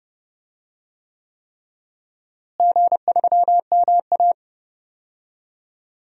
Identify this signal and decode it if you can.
Morse code, heard as a keyed tone, about 30 wpm, 700 Hz. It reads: G3MA
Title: DXCC-100 Call Signs - 4 Characters - Repeated 3x - 30wpm